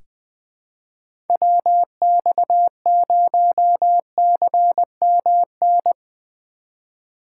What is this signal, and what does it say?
Morse code, heard as a keyed tone, about 20 wpm, 700 Hz. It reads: WX0CMN
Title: USA Call Signs - 4x Character Spacing - 20wpm